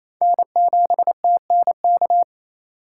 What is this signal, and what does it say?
Morse code, heard as a keyed tone, about 28 wpm, 700 Hz. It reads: N7TNK
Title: USA Call Signs - 5 Characters - 4x Character Spacing - 28wpm